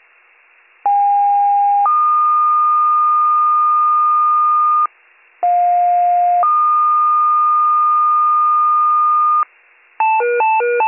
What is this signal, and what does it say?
Two-tone page, heard as a one-second tone, then a three-second tone, about 810 then 1220 Hz, about 710 then 1220 Hz. It is heard from 0.8 to 9.4 s.